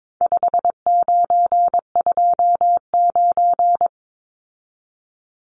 Morse code, heard as a keyed tone, about 22 wpm, 700 Hz. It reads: 5929
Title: Numbers - 4-Digit 22wpm